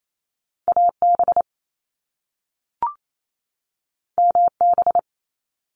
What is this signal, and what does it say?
Morse code, heard as a keyed tone, about 28 wpm, 700 Hz. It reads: AB MB
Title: Canadian Provinces and Territories 28wpm